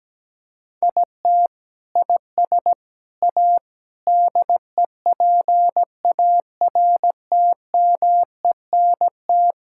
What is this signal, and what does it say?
Morse code, heard as a keyed tone, about 17 wpm, 700 Hz. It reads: IT IS A DEPARTMENT
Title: Sentences from Top 1000 Words - Review Part 5 - 17wpm